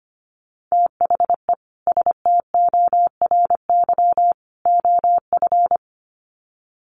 Morse code, heard as a keyed tone, about 25 wpm, 700 Hz. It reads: THE STORY OF
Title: Sets of 3 Words 25wpm